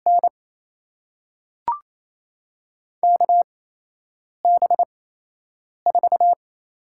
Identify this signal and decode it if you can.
Morse code, heard as a keyed tone, about 28 wpm, spaced as for 8 wpm, 700 Hz. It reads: N KB4
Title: USA Call Signs - 6 Characters - 8x Character Spacing - 28wpm